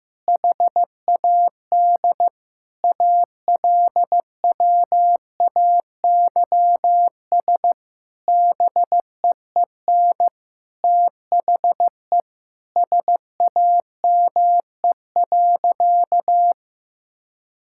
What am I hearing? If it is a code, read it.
Morse code, heard as a keyed tone, about 15 wpm, 700 Hz. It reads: HAD ALWAYS BEEN THE SAME.
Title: Sentences from Top 1000 Words - Review Part 6 - 15wpm